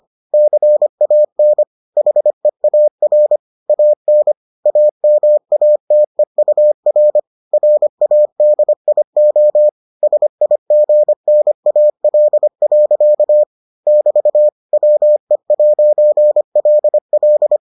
Morse code, heard as a keyed tone, about 25 wpm, 600 Hz. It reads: CAN HEAR AN AMATEUR RADIO SIGNAL. <BT> WE'LL